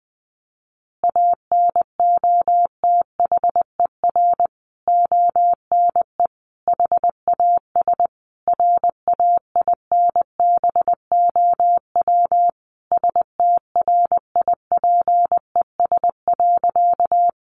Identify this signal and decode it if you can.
Morse code, heard as a keyed tone, about 20 wpm, 700 Hz. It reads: ANOTHER ONE HAS RAINBOW STRIPES.
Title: Additional Easy Phrases 20wpm